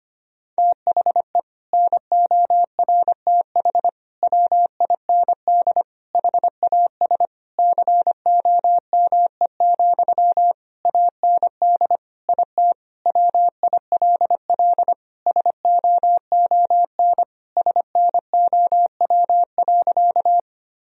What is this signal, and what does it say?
Morse code, heard as a keyed tone, about 25 wpm, 700 Hz. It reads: THE NORTH WIND HAS COME, AND IT WILL SOON SNOW.
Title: Sentences from Top 1000 Words - Review Part 3 - 25wpm